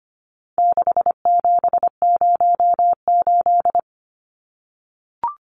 Morse code, heard as a keyed tone, about 25 wpm, 700 Hz. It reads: 6708
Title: Numbers - 4-Digit 25wpm